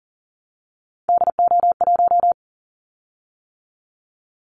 Morse code, heard as a keyed tone, about 41 wpm, 700 Hz. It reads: DO1
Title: Callsign Prefixes for Top 50 DXCC - 40wpm